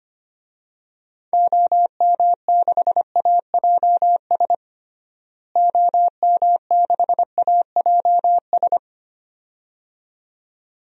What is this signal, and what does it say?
Morse code, heard as a keyed tone, about 25 wpm, 700 Hz. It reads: OM6AJS OM6AJS
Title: DXCC-100 Call Signs - 6 Characters - Repeated 2x - 25wpm